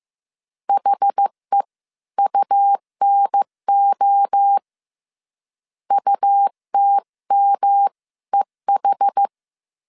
Morse code, heard as a keyed tone, about 15 wpm, spaced as for 13 wpm, 800 Hz. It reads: HEUNO UTMEH
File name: CW / DET / 1 3 1500C S C 14wpm 2458